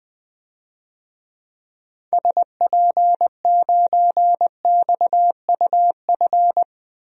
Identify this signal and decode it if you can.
Morse code, heard as a keyed tone, about 20 wpm, 700 Hz. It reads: SP9XUF
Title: DXCC-100 Call Signs - 6 Characters - Repeated 2x - 20wpm